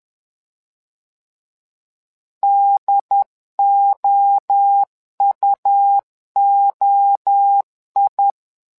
Morse code, heard as a keyed tone, about 11 wpm, 800 Hz. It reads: DOUOI
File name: LEZIONI / 3 MWRO / 3 MWRO 10wpm 2148